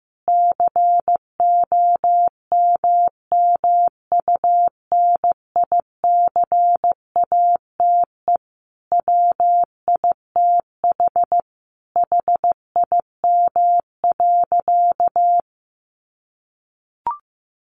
Morse code, heard as a keyed tone, about 15 wpm, 700 Hz. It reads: COMMUNICATE WITH HIM.
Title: Sentences from Top 1300 Words 15wpm